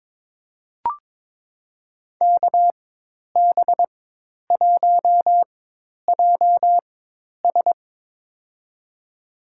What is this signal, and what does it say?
Morse code, heard as a keyed tone, about 21 wpm, spaced as for 10 wpm, 700 Hz. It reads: KB1JS
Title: USA Call Signs - 5 Characters - 4x Character Spacing - 22wpm